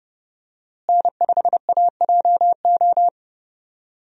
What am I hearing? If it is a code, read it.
Morse code, heard as a keyed tone, about 30 wpm, 700 Hz. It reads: N5AJO
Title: USA Call Signs - 5 Characters - Repeated 3x - 30wpm